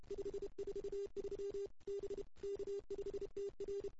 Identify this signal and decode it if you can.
Morse code, heard as a keyed tone, about 29 wpm, 410 Hz. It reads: 543 B K5TR
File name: CW / 2008-ss-cw-k5tr-op-n5rz -02:00